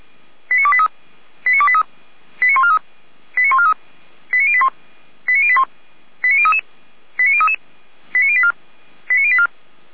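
Five-tone selective call (ZVEI), heard as quick runs of five tones, tones of about 70 ms. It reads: 89283 89283 89143 89143 89081 89081 89033 89033 89084 89084
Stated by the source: ZVEI five-tone selective call received by SDR.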